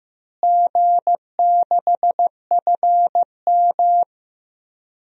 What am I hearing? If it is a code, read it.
Morse code, heard as a keyed tone, about 15 wpm, 700 Hz. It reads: G6FM
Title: DXCC-100 Call Signs - 4 Characters - Repeated 3x - 15wpm